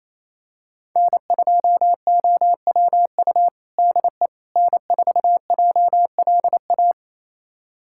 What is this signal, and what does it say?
Morse code, heard as a keyed tone, about 28 wpm, 700 Hz. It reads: N2OWU DE N4JLA
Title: USA Callsign DE Callsign - 5 Characters - 28wpm